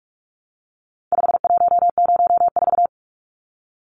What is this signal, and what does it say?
Morse code, heard as a keyed tone, about 44 wpm, 700 Hz. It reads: 5104
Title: Numbers - 4-Digit 45wpm